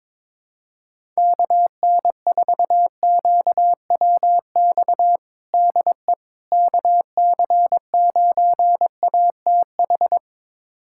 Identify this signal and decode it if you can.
Morse code, heard as a keyed tone, about 22 wpm, 700 Hz. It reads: KN4QWX DE KC9ATH
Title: USA Callsign DE Callsign - 6 Characters - 22wpm